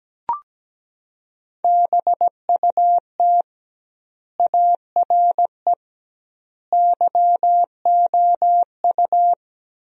Morse code, heard as a keyed tone, about 17 wpm, 700 Hz. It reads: BUT ARE YOU
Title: 2x Word-Spacing - Sentences from Top 300 Words - 17wpm